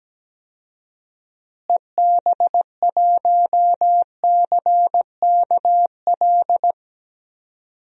Morse code, heard as a keyed tone, about 17 wpm, 700 Hz. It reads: EB1CKL
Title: DXCC-100 Call Signs - 6 Characters - Repeated 2x - 17wpm